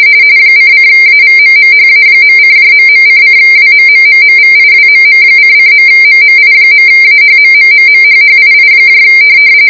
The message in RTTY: ::AVMMHA.:..::WMMMMVIWMM::.MMI.AHMMMMHHII